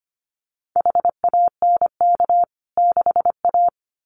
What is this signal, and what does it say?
Morse code, heard as a keyed tone, about 25 wpm, 700 Hz. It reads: HANK 6A
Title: CWT Exchange 25wpm